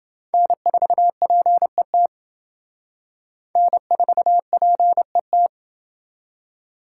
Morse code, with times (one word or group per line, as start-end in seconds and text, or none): 0.3-2.1 N4PET
3.6-5.5 N4PET